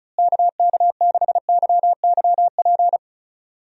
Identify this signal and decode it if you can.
Morse code, heard as a keyed tone, about 36 wpm, 700 Hz. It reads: KK6YYP
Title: USA Call Signs - 6 Characters - Repeated 2x - 35wpm